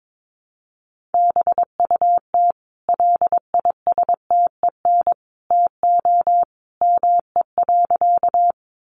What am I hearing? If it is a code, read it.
Morse code, heard as a keyed tone, about 22 wpm, 700 Hz. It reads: BUT LISTEN TO ME.
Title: Sentences from Top 400 Words 22wpm